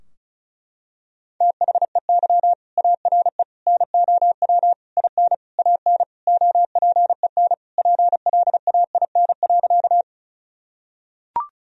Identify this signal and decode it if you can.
Morse code, heard as a keyed tone, about 34 wpm, 700 Hz. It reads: THEY ARE NOW IN AN OPEN PLAIN.
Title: Sentences from Top 1000 Words - Review Part 2 - 35wpm